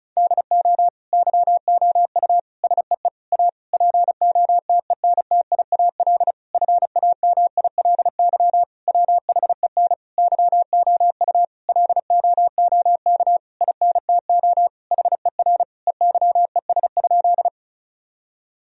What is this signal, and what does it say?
Morse code, heard as a keyed tone, about 36 wpm, 700 Hz. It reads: DO YOU SEE A POTENTIAL FAMILY WHEN YOU LOOK INTO HER EYES?
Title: Sentences from Top 1000 Words - Review Part 4 - 35wpm